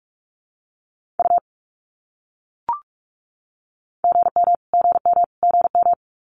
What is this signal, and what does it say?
Morse code, heard as a keyed tone, about 44 wpm, 700 Hz. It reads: U GM GM GM
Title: Abbreviations Repeated 3x - 45wpm